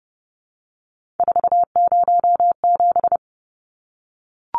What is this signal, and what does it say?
Morse code, heard as a keyed tone, about 30 wpm, 700 Hz. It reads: 407
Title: Numbers - 3-Digit 30wpm